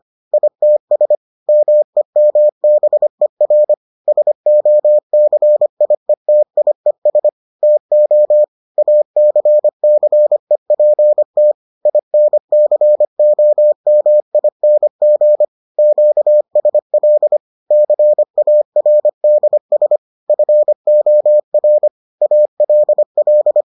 Morse code, heard as a keyed tone, about 25 wpm, 600 Hz. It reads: ITS MEMBER SOCIETIES TO ACCEPT INCOMING QSL CARDS FOR ALL